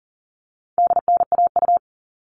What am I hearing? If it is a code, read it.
Morse code, heard as a keyed tone, about 40 wpm, 700 Hz. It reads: DNAU
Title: All Cut Numbers - 4-Digit 40wpm